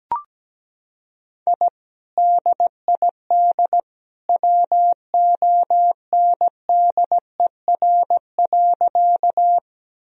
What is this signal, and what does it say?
Morse code, heard as a keyed tone, about 17 wpm, 700 Hz. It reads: I DID WONDER.
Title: Sentences from Top 1000 Words - Review Part 3 - 17wpm